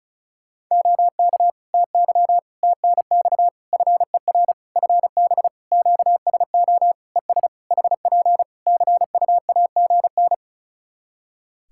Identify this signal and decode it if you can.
Morse code, heard as a keyed tone, about 36 wpm, 700 Hz. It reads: OK TY TNX FER FB QSO ES HP CUAGN
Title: K1USN SST QSO Practice 35wpm